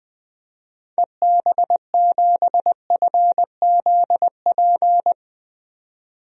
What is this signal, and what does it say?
Morse code, heard as a keyed tone, about 20 wpm, 700 Hz. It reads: EB7FZP
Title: DXCC-100 Call Signs - 6 Characters - 20wpm